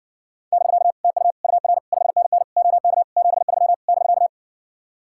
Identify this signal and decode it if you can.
Morse code, heard as a keyed tone, about 60 wpm, 700 Hz. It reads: <BK> TU UR 5NN OK 73 <BK>